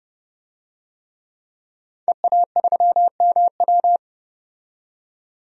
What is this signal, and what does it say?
Morse code, heard as a keyed tone, about 30 wpm, 700 Hz. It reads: EA3MW
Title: DXCC-100 Call Signs - 5 Characters - 30wpm